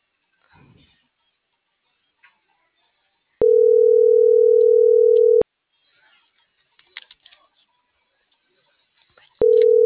Ringback tone (North America), 440 + 480 Hz.